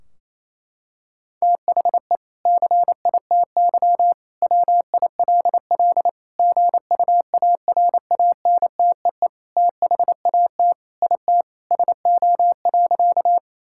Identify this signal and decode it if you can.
Morse code, heard as a keyed tone, about 28 wpm, 700 Hz. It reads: THE CITY WILL GUARANTEE THAT IT SO.
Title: Sentences from Top 1600 Words 28wpm